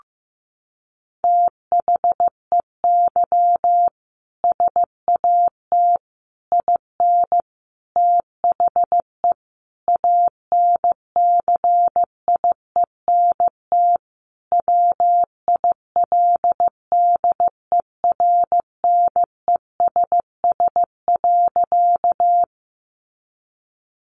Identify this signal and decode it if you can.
Morse code, heard as a keyed tone, about 14 wpm, 700 Hz. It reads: THEY SAT IN THE ANCIENT WILDERNESS.
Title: A Sound Of Thunder 15wpm